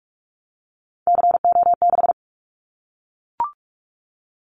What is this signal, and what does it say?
Morse code, heard as a keyed tone, about 44 wpm, 700 Hz. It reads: CO6
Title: Callsign Prefixes for Top 100 DXCC - 45wpm